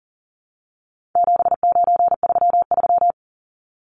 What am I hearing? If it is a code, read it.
Morse code, heard as a keyed tone, about 40 wpm, 700 Hz. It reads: 7933
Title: Numbers - 4-Digit 40wpm